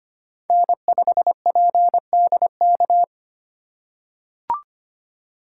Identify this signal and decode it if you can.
Morse code, heard as a keyed tone, about 25 wpm, 700 Hz. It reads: N5PDK
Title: USA Call Signs - 5 Characters - 2x Character Spacing - 25wpm